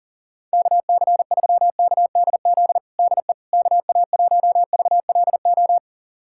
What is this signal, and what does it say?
Morse code, heard as a keyed tone, about 40 wpm, 700 Hz. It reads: KC3KDZ DE KA1VLO
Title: USA Callsign DE Callsign - 6 Characters - 40wpm